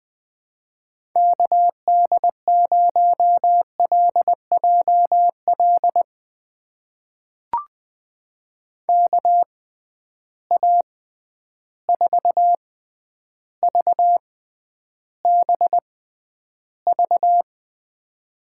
Morse code, heard as a keyed tone, about 20 wpm, spaced as for 10 wpm, 700 Hz. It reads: KD0LJL KA4VBV
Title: USA Call Signs - 6 Characters - 6x Character Spacing - 20wpm